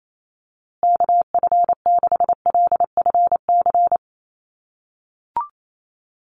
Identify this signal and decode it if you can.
Morse code, heard as a keyed tone, about 28 wpm, 700 Hz. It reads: KF6LFC
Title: USA Call Signs Repeated 3x 28wpm